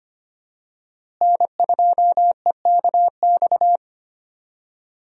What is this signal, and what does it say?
Morse code, heard as a keyed tone, about 25 wpm, 700 Hz. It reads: N2EKX
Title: USA Call Signs - 5 Characters - Repeated 2x - 25wpm